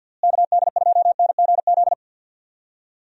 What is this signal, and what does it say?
Morse code, heard as a keyed tone, about 50 wpm, 700 Hz. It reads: KD2NGZ
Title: USA Call Signs - 6 Characters - Repeated 2x - 50wpm